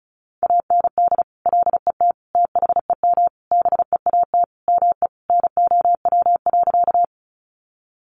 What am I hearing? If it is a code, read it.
Morse code, heard as a keyed tone, about 34 wpm, 700 Hz. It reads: AND LET THEM BEAT ME NOW.
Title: Sentences from Top 1300 Words 35wpm